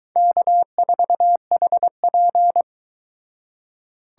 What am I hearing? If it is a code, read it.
Morse code, heard as a keyed tone, about 22 wpm, 700 Hz. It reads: K4HP